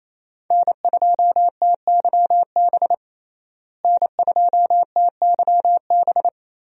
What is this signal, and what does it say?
Morse code, heard as a keyed tone, about 28 wpm, 700 Hz. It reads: N2TYB N2TYB
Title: USA Call Signs - 5 Characters - Repeated 3x - 28wpm